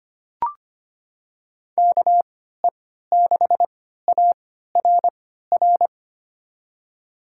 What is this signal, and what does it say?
Morse code, heard as a keyed tone, about 25 wpm, spaced as for 14 wpm, 700 Hz. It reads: KE6ARR
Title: USA Call Signs - 6 Characters - 3x Character Spacing - 25wpm